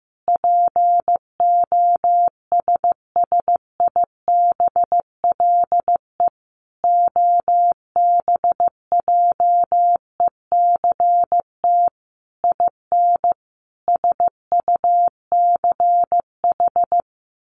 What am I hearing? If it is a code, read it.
Morse code, heard as a keyed tone, about 15 wpm, 700 Hz. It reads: POSSIBLE OBJECT IN SUCH
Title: Sentences from Top 1500 Words 15wpm